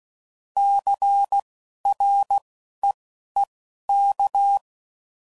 Morse code, heard as a keyed tone, about 16 wpm, spaced as for 11 wpm, 800 Hz. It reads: CREEK
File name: CW Code Groups @10WPM / fromSM's / Cities